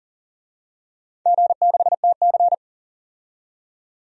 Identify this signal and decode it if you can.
Morse code, heard as a keyed tone, about 41 wpm, 690 Hz. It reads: G6TC
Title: DXCC-100 Call Signs - 4 Characters - 40wpm